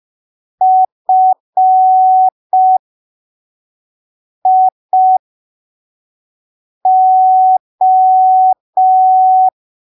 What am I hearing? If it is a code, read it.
Morse code, heard as a keyed tone, about 5 wpm, spaced as for 3 wpm, 750 Hz. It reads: FIO
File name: arquivos / texto / texto 13 05ppm